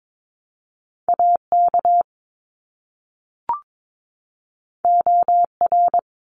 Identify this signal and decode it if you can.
Morse code, heard as a keyed tone, about 22 wpm, 700 Hz. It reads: AK OR
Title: ARRL Sections 22wpm